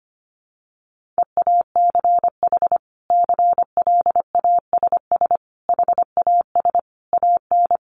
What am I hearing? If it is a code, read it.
Morse code, heard as a keyed tone, about 25 wpm, 700 Hz. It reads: EACH CLASS HAS AN
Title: Sentences from Top 1400 Words 25wpm